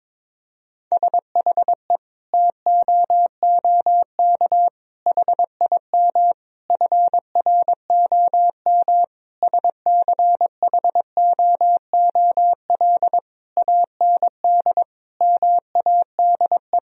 Morse code, heard as a keyed tone, about 22 wpm, 700 Hz. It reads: SHE TOOK HIM FROM SCHOOL AND MADE
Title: Sentences from Top 1000 Words - Review Part 4 - 22wpm